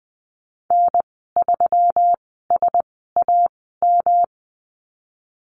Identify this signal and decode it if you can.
Morse code, heard as a keyed tone, about 20 wpm, spaced as for 15 wpm, 700 Hz. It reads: N3SAM
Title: 122 Lesson 20g - Number 3 - Call Signs 20wpm